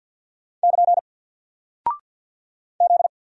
Morse code, heard as a keyed tone, about 48 wpm, 700 Hz. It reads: <KN> Z
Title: Single Character (Letter:Number:Punctuation:Prosign) Alt 50wpm